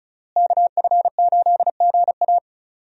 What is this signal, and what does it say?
Morse code, heard as a keyed tone, about 35 wpm, 700 Hz. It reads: KF8GA
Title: USA Call Signs - 5 Characters - Repeated 2x - 35wpm